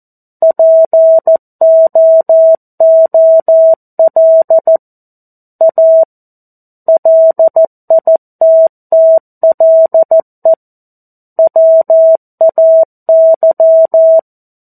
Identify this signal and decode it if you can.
Morse code, heard as a keyed tone, about 14 wpm, 650 Hz. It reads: POOL A LITTLE WAY
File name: 02-Chapter-02 14 WPM